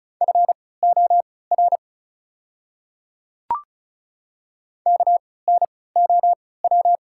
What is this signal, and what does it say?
Morse code, heard as a keyed tone, about 34 wpm, spaced as for 20 wpm, 690 Hz. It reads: FOR KNOW
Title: 3x Character Spacing - Top 100 Words - 35wpm